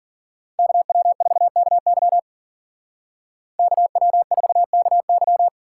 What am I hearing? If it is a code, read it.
Morse code, heard as a keyed tone, about 49 wpm, 700 Hz. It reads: KW4KY KW4KY
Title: Speed Racing - USA Call Signs 40wpm